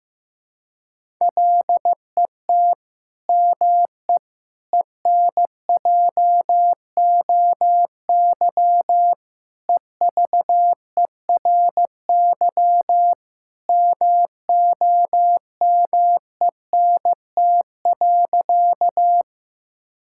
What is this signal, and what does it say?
Morse code, heard as a keyed tone, about 15 wpm, 700 Hz. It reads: LET ME ENJOY EVERY MOMENT.
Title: Sentences from Top 1000 Words - Review Part 5 - 15wpm